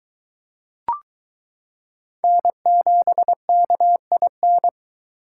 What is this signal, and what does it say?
Morse code, heard as a keyed tone, about 22 wpm, 700 Hz. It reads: N7KIN